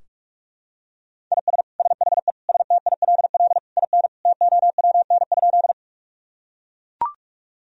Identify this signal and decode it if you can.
Morse code, heard as a keyed tone, about 42 wpm, 700 Hz. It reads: IS SHE STILL IN TOWN?
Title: Sentences from Top 1000 Words - Review Part 2 - 45wpm